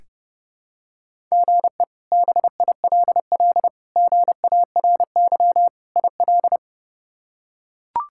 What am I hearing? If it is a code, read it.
Morse code, heard as a keyed tone, about 31 wpm, 700 Hz. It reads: GE BILL GARY IL
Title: K1USN SST QSO Practice - 30wpm